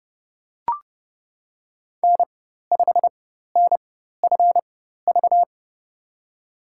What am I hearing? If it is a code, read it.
Morse code, heard as a keyed tone, about 32 wpm, spaced as for 14 wpm, 700 Hz. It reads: N5NFV
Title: USA Call Signs - 5 Characters - 4x Character Spacing - 30wpm